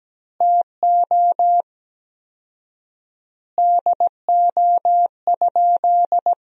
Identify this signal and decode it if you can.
Morse code, heard as a keyed tone, about 17 wpm, 700 Hz. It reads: TO DO?
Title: 4x Word-Spacing - Sentences from Top 400 Words - 17wpm